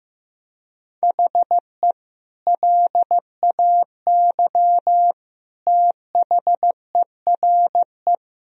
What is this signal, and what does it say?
Morse code, heard as a keyed tone, about 15 wpm, 700 Hz. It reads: HE LAY THERE
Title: Sentences from Top 1000 Words - Review Part 2 - 15wpm